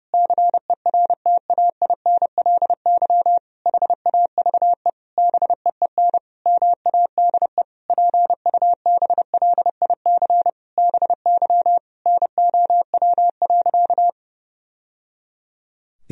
Morse code, heard as a keyed tone, about 30 wpm, 700 Hz. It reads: CERTAINLY HAVE BEEN MADE PUBLIC BY NOW.
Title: Sentences from Top 1000 Words - Review Part 3 - 30wpm